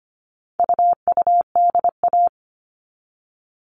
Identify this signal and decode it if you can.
Morse code, heard as a keyed tone, about 25 wpm, 700 Hz. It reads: UUDA
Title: All Cut Numbers - 4-Digit 25wpm